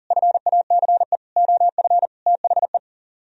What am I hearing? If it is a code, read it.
Morse code, heard as a keyed tone, about 52 wpm, 700 Hz. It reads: FACE OF THE